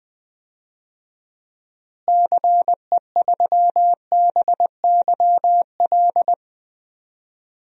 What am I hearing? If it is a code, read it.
Morse code, heard as a keyed tone, about 21 wpm, 700 Hz. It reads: CE3BYL
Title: DXCC-100 Call Signs - 6 Characters - Repeated 3x - 20wpm